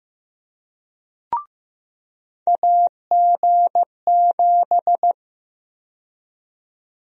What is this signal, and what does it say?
Morse code, heard as a keyed tone, about 15 wpm, 700 Hz. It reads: AG7
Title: USA Call Sign Prefixes - 15wpm